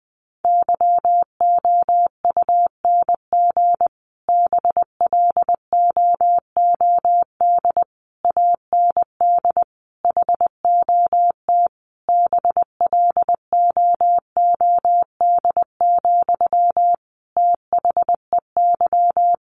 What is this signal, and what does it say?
Morse code, heard as a keyed tone, about 20 wpm, 700 Hz. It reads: YOUNG BLOOD AND HOT BLOOD, THEY
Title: Sentences from Top 1100 Words 20wpm